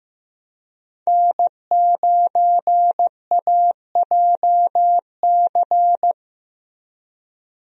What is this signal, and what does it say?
Morse code, heard as a keyed tone, about 15 wpm, 700 Hz. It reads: N9AJC
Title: USA Call Signs - 5 Characters - 15wpm